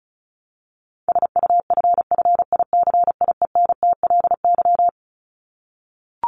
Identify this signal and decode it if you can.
Morse code, heard as a keyed tone, about 34 wpm, 700 Hz. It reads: SUFFICIENTLY
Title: LY Suffix - Top 100 Words with Base Words 35wpm